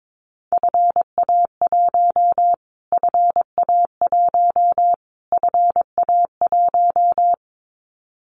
Morse code, heard as a keyed tone, about 22 wpm, 700 Hz. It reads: FA1 FA1 FA1
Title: Callsign Prefixes for Top 25 DXCC - Repeated 3x - 22wpm